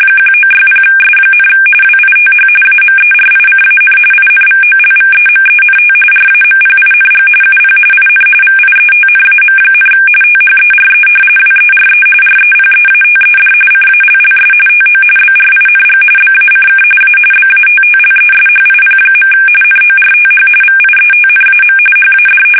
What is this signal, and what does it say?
RTTY, about 45.45 baud. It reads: 1960S, WITH AMERICAN INVOLVEMENT RAMPING UP IN VIETNAM, THERE WERE REPEATED CALLS TO BRING BACK A BATTLESHIP. NORTH VIETNAM HAD
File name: fsk850 / battleshipnj